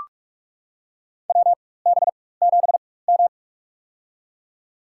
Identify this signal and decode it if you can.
Morse code, heard as a keyed tone, about 41 wpm, spaced as for 20 wpm, 700 Hz. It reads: WB7M
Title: USA Call Signs - 4 Characters - 4x Character Spacing - 45wpm